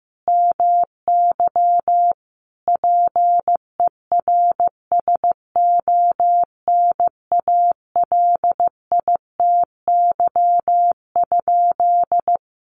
Morse code, heard as a keyed tone, about 15 wpm, 700 Hz. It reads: MY PERSONALITY?
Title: Sentences from Top 1400 Words 15wpm